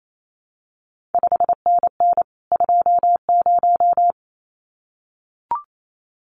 Morse code, heard as a keyed tone, about 28 wpm, 700 Hz. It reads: 5NN 20
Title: CQ WW DX Contest Exchange - 28wpm